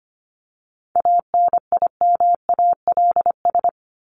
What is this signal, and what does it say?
Morse code, heard as a keyed tone, about 25 wpm, 700 Hz. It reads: ANIMALS
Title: ES:S Suffix - Top 100 Words with Base Words 25wpm